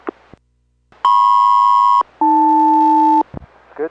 Aircraft SELCAL, heard as two pairs of tones. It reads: MP-AL